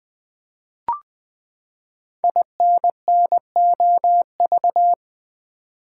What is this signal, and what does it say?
Morse code, heard as a keyed tone, about 20 wpm, 700 Hz. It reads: INNOV
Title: Build A Word - 5 Letters At A Time - 10-Letter Words - 20wpm